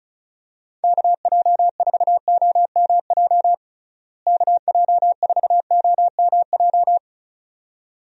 Morse code, heard as a keyed tone, about 34 wpm, 700 Hz. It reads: KJ4OMJ KJ4OMJ
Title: USA Call Signs - 6 Characters - Repeated 2x - 35wpm